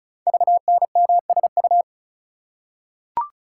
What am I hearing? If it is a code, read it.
Morse code, heard as a keyed tone, about 36 wpm, 690 Hz. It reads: VNMSU
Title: Code Groups - 5 Letters - 35wpm